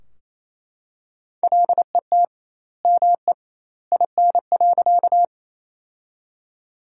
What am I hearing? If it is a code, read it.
Morse code, heard as a keyed tone, about 27 wpm, 700 Hz. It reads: LET ME IN.
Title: 2x Word-Spacing - Sentences from Top 300 Words - 28wpm